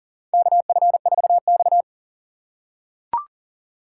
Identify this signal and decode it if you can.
Morse code, heard as a keyed tone, about 40 wpm, 700 Hz. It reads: KF4X
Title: USA Call Signs - 4 Characters - 2x Character Spacing - 40wpm